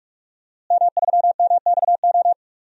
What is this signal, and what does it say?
Morse code, heard as a keyed tone, about 41 wpm, 700 Hz. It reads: M3MXO
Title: DXCC-100 Call Signs - 5 Characters - Repeated 2x - 45wpm